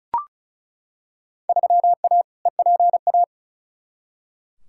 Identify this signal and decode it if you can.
Morse code, heard as a keyed tone, about 36 wpm, 700 Hz. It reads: 3A EPA
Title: Field Day Exchanges - 35wpm